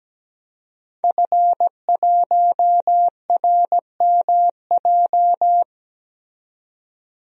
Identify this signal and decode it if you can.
Morse code, heard as a keyed tone, about 17 wpm, 700 Hz. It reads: F1RMJ
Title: DXCC-100 Call Signs - 5 Characters - 17wpm